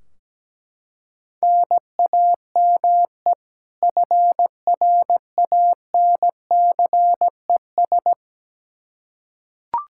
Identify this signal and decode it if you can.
Morse code, heard as a keyed tone, about 17 wpm, 700 Hz. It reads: NAME FRANCES
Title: Practice QSO - Exchange 1 - Beginning - 17wpm